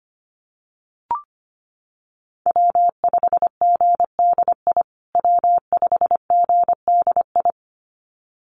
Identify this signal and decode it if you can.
Morse code, heard as a keyed tone, about 25 wpm, 700 Hz. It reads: W5GDI W5GDI
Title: USA Call Signs Repeated 2x 25wpm